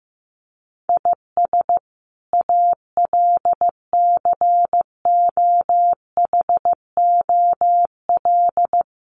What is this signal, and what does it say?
Morse code, heard as a keyed tone, about 15 wpm, 700 Hz. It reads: IS ALCOHOL
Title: Sentences from Top 1400 Words 15wpm